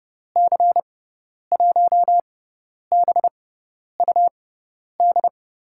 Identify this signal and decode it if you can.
Morse code, heard as a keyed tone, about 30 wpm, spaced as for 10 wpm, 700 Hz. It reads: C1BUD
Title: USA Call Signs - 6 Characters - 6x Character Spacing - 30wpm